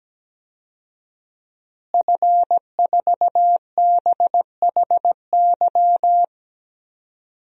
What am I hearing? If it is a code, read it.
Morse code, heard as a keyed tone, about 17 wpm, 700 Hz. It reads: F4BHY
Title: DXCC-100 Call Signs - 5 Characters - Repeated 3x - 17wpm